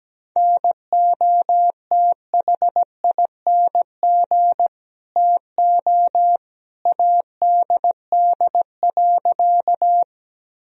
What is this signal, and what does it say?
Morse code, heard as a keyed tone, about 17 wpm, 700 Hz. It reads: NOTHING TO ADD.
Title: Sentences from Top 1000 Words - Review Part 1 - 17wpm